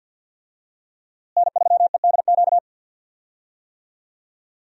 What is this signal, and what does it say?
Morse code, heard as a keyed tone, about 53 wpm, 700 Hz. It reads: N3EDQ
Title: USA Call Signs - 5 Characters - Repeated 2x - 50wpm